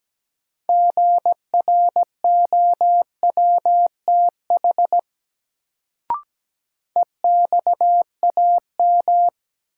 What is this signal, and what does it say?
Morse code, heard as a keyed tone, about 17 wpm, 700 Hz. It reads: GROWTH EXAM
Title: Sentences from Top 1000 Words - Review Part 5 - New Word Drill - 17wpm